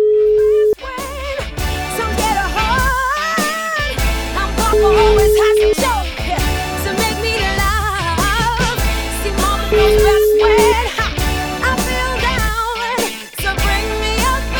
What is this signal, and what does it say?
Ringback tone (Europe, 425 Hz), with noise.